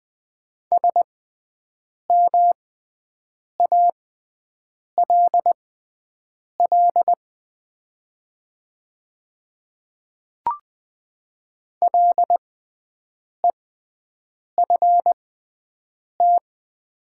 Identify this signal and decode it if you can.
Morse code, heard as a keyed tone, about 20 wpm, spaced as for 7 wpm, 700 Hz. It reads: SMALL LEFT
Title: 6x Character Spacing - Top 300 Words - 20wpm